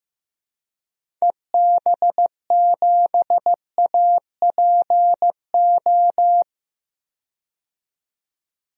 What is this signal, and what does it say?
Morse code, heard as a keyed tone, about 15 wpm, 700 Hz. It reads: EB7APO
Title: DXCC-100 Call Signs - 6 Characters - 15wpm